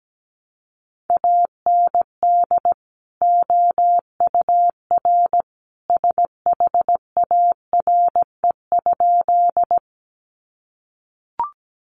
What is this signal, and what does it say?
Morse code, heard as a keyed tone, about 17 wpm, 700 Hz. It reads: AND OUR SHARE?
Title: Sentences from Top 900 Words 17wpm